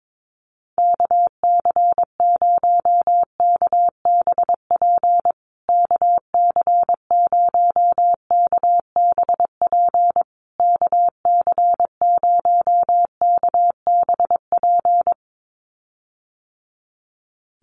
Morse code, heard as a keyed tone, about 22 wpm, 700 Hz. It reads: KC0KBP KC0KBP KC0KBP
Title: USA Call Signs Repeated 3x 22wpm